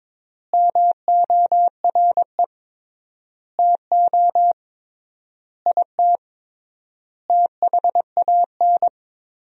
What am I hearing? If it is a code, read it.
Morse code, heard as a keyed tone, about 22 wpm, 700 Hz. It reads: MORE TO IT THAN
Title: 3x Word-Spacing - Sentences from Top 100 Words - 22wpm